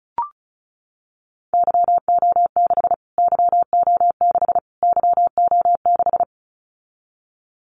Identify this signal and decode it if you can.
Morse code, heard as a keyed tone, about 34 wpm, 700 Hz. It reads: YO6 YO6 YO6
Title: Callsign Prefixes for Top 25 DXCC - Repeated 3x - 35wpm